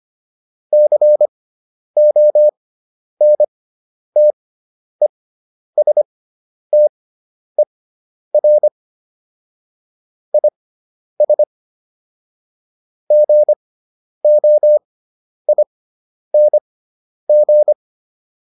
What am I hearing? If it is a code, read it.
Morse code, heard as a keyed tone, about 25 wpm, spaced as for 10 wpm, 600 Hz. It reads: CONTESTER IS GOING